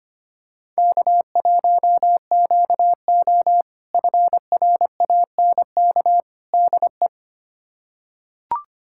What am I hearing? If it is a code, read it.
Morse code, heard as a keyed tone, about 25 wpm, 700 Hz. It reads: K1QO FRANK DE
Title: K1USN SST QSO Practice - 25wpm